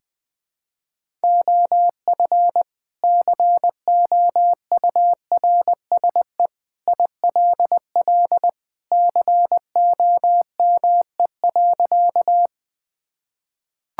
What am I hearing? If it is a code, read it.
Morse code, heard as a keyed tone, about 20 wpm, 700 Hz. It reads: OF COURSE ILL COME.
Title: Sentences with 2-Word Phrases - 20wpm